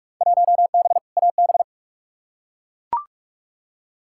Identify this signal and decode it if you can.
Morse code, heard as a keyed tone, about 41 wpm, 700 Hz. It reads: 1B AB
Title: Field Day Exchanges - 45wpm